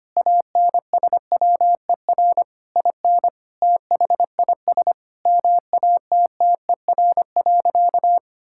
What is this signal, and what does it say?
Morse code, heard as a keyed tone, about 25 wpm, 700 Hz. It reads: ANSWER IN THIS MATTER.
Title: Sentences from Top 1600 Words 25wpm